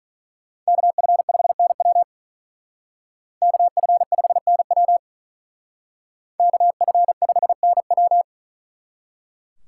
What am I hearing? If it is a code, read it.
Morse code, heard as a keyed tone, about 41 wpm, 700 Hz. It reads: KF5NW KF5NW KF5NW
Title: Speed Racing - USA 2x2 Call Signs 35wpm